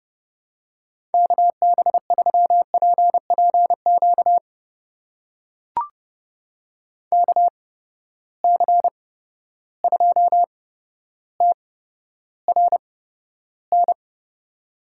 Morse code, heard as a keyed tone, about 31 wpm, spaced as for 13 wpm, 700 Hz. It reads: KB3PPQ KC2TRN
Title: USA Call Signs - 6 Characters - 8x Character Spacing - 30wpm